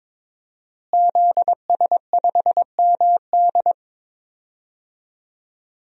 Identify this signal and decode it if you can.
Morse code, heard as a keyed tone, about 22 wpm, 700 Hz. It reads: ZS5MD
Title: DXCC-100 Call Signs - 5 Characters - Repeated 2x - 22wpm